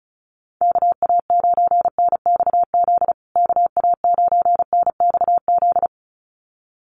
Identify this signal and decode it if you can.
Morse code, heard as a keyed tone, about 34 wpm, 700 Hz. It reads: KA9NXZ KA9NXZ
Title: USA Call Signs Repeated 2x 35wpm